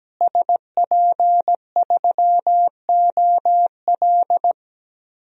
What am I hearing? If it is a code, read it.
Morse code, heard as a keyed tone, about 17 wpm, 700 Hz. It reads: SP3OL
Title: DXCC-100 Call Signs - 5 Characters - Repeated 2x - 17wpm